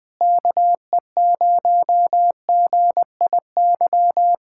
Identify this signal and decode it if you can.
Morse code, heard as a keyed tone, about 20 wpm, 700 Hz. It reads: KE0GIY
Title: Speed-Racing Common QSO Phrases - 17wpm